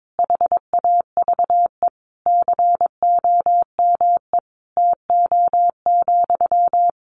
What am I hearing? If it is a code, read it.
Morse code, heard as a keyed tone, about 22 wpm, 700 Hz. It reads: HAVE COME TO,
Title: Sentences from Top 1000 Words 22wpm